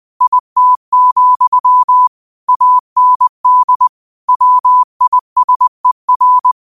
Morse code, heard as a keyed tone, about 20 wpm, 1000 Hz. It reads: IT, AND WISER
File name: qotdfiles / 20wpm